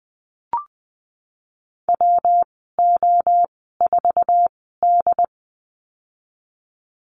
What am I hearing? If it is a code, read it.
Morse code, heard as a keyed tone, about 19 wpm, spaced as for 14 wpm, 700 Hz. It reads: WO4D
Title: 207 Lesson 33g - Letter j - Call Signs 20wpm